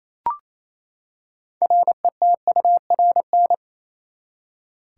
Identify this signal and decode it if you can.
Morse code, heard as a keyed tone, about 28 wpm, 700 Hz. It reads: RETURN
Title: Six Letter Words - 28wpm